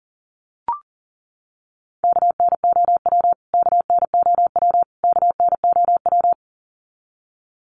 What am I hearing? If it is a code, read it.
Morse code, heard as a keyed tone, about 40 wpm, 700 Hz. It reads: KNOW KNOW KNOW
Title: Top 300 Words Repeated 40wpm